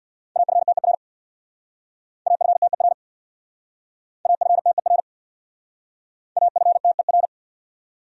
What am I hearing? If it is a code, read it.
Morse code, heard as a keyed tone, about 67 wpm, 700 Hz. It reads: AFTER AFTER AFTER AFTER